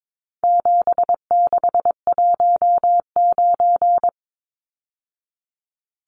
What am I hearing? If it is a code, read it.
Morse code, heard as a keyed tone, about 22 wpm, 700 Hz. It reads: 7619
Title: Numbers - 4-Digit 22wpm